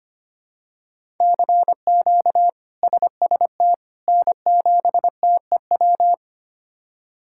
Morse code, heard as a keyed tone, about 25 wpm, 700 Hz. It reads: CQ SST N7TEW
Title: K1USN SST QSO Practice - 25wpm